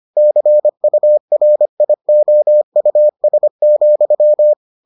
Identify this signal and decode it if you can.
Morse code, heard as a keyed tone, about 25 wpm, 600 Hz. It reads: CURIOUS,